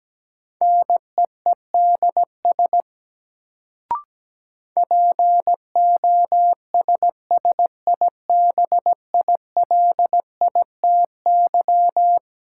Morse code, heard as a keyed tone, about 17 wpm, 700 Hz. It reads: NEEDS POSSIBILITY
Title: Sets of 3 Words - New Word Drill - 17wpm